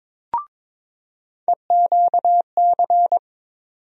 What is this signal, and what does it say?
Morse code, heard as a keyed tone, about 22 wpm, 700 Hz. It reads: EQC
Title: USA Call Sign Postfixes - 3 Letters 22wpm